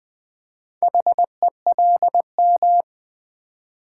Morse code, heard as a keyed tone, about 20 wpm, 700 Hz. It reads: HELM
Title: Build A Word - 2 Letters At A Time - 6-Letter Words - 20wpm